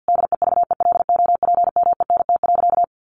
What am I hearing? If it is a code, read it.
Morse code, heard as a keyed tone, about 48 wpm, 700 Hz. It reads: DEVELOPMENT.
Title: Sentences from Top 1400 Words 50wpm